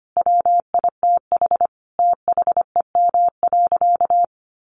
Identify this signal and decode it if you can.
Morse code, heard as a keyed tone, about 25 wpm, 700 Hz. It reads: WITH THEM.
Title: Sentences from Top 700 Words 25wpm